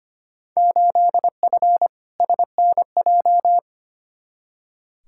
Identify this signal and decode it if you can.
Morse code, heard as a keyed tone, about 25 wpm, 700 Hz. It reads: 8F SNJ
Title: Field Day Exchanges Encore - 25wpm